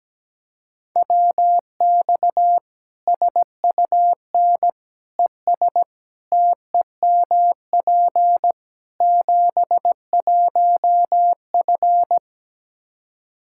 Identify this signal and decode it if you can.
Morse code, heard as a keyed tone, about 17 wpm, 700 Hz. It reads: WX SUN ES TEMP 71F
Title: Practice QSO - Exchange 2 - Info - 17wpm